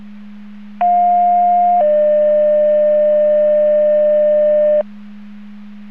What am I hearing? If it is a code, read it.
Two-tone page, heard as a one-second tone, then a three-second tone, about 710 then 600 Hz. It is heard from 0.8 to 4.8 s.